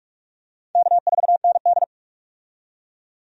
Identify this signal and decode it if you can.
Morse code, heard as a keyed tone, about 46 wpm, 700 Hz. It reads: K4ND
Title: USA Call Signs - 4 Characters - 45wpm